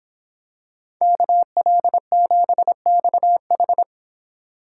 Morse code, heard as a keyed tone, about 24 wpm, 700 Hz. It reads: KL7XH